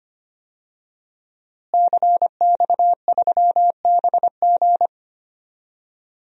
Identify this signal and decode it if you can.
Morse code, heard as a keyed tone, about 25 wpm, 700 Hz. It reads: CX3BG
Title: DXCC-100 Call Signs - 5 Characters - Repeated 2x - 25wpm